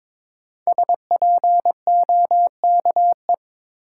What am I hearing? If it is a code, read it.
Morse code, heard as a keyed tone, about 21 wpm, 700 Hz. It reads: SPOKE
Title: Sentences from Top 1200 Words - New Word Drill - 22wpm